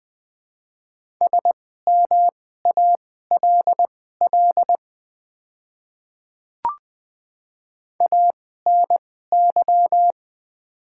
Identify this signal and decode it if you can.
Morse code, heard as a keyed tone, about 20 wpm, spaced as for 14 wpm, 690 Hz. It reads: SMALL ANY
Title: 2x Character Spacing - Top 200 Words - 20wpm